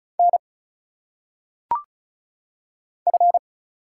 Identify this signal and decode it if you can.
Morse code, heard as a keyed tone, about 36 wpm, 700 Hz. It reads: N F
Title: Single Letters - Phonetic Pronunciation 35wpm